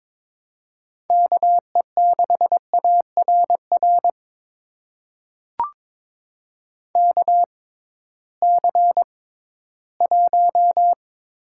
Morse code, heard as a keyed tone, about 21 wpm, spaced as for 14 wpm, 700 Hz. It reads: KE6ARR KC1
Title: USA Call Signs - 6 Characters - 6x Character Spacing - 22wpm